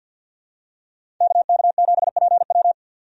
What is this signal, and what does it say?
Morse code, heard as a keyed tone, about 50 wpm, 700 Hz. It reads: KK7PW
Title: USA Call Signs - 5 Characters - 50wpm